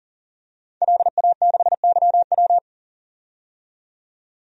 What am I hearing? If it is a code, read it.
Morse code, heard as a keyed tone, about 40 wpm, 700 Hz. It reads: LA6YW
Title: DXCC-100 Call Signs - 5 Characters - 40wpm